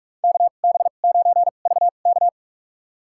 Morse code, heard as a keyed tone, about 45 wpm, spaced as for 32 wpm, 700 Hz. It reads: KB9VK
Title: USA Call Signs - 5 Characters - 2x Character Spacing - 45wpm